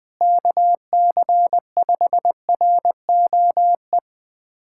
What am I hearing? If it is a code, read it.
Morse code, heard as a keyed tone, about 20 wpm, 700 Hz. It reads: KC5ROE
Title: USA Callsign DE Callsign - 6 Characters - 20wpm